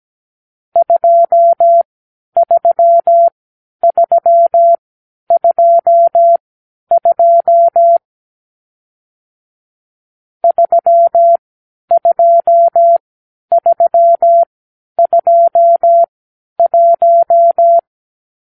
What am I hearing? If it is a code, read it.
Morse code, heard as a keyed tone, about 17 wpm, spaced as for 10 wpm, 680 Hz. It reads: 23322 32321
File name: k7qo / Lesson 44 (1-3)